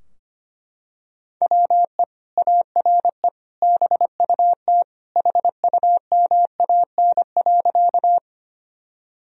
Morse code, heard as a keyed tone, about 25 wpm, 700 Hz. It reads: WE ARE BUT HUMAN.
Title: Sentences from Top 1000 Words - Review Part 3 - 25wpm